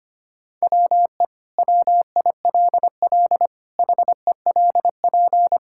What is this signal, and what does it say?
Morse code, heard as a keyed tone, about 25 wpm, 700 Hz. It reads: WE WILL HELP
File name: practice / 25wpm / Sentences with 2-Word Phrases - 25wpm